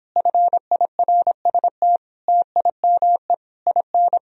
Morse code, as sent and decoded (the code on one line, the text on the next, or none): ..-. .. .-. ... - / - .. -- . / .. -.
FIRST TIME IN